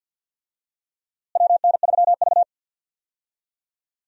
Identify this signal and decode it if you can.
Morse code, heard as a keyed tone, about 49 wpm, 700 Hz. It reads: WN3V
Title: USA Call Signs - 4 Characters - Repeated 3x - 50wpm